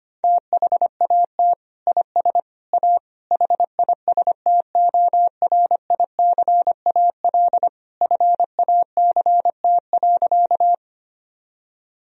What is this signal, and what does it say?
Morse code, heard as a keyed tone, about 25 wpm, 700 Hz. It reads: THAT IS A HISTORICAL FACT.
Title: Sentences from Top 1000 Words - Review Part 4 - 25wpm